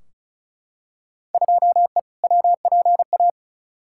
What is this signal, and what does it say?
Morse code, heard as a keyed tone, about 34 wpm, 700 Hz. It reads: 2E WPA
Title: Field Day Exchanges - 35wpm